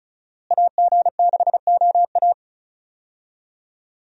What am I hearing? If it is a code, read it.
Morse code, heard as a keyed tone, about 34 wpm, 700 Hz. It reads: AG6OA
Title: USA Call Signs - 5 Characters - Repeated 2x - 35wpm